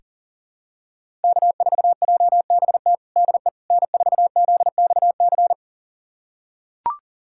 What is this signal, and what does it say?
Morse code, heard as a keyed tone, about 40 wpm, 700 Hz. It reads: K4JBT DE N4ZXC
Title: USA Callsign DE Callsign - 5 Characters - 40wpm